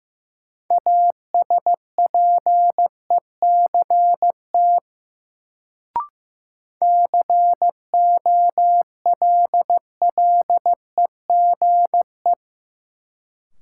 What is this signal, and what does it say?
Morse code, heard as a keyed tone, about 15 wpm, 700 Hz. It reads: ASPECT COLLEGE
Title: Sentences from Top 1300 Words - New Word Drill - 15wpm